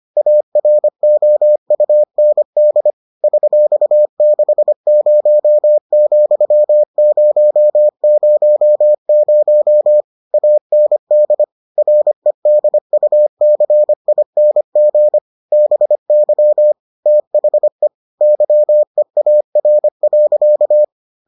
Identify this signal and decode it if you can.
Morse code, heard as a keyed tone, about 25 wpm, 600 Hz. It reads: AROUND $60,000 AND REDUCING BY THE YEAR.